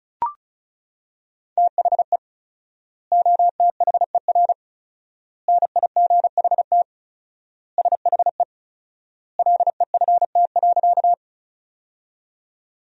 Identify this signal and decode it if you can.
Morse code, heard as a keyed tone, about 36 wpm, 700 Hz. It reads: THE OTHER NIGHT SHE LEFT.
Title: 4x Word-Spacing - Sentences from Top 300 Words - 35wpm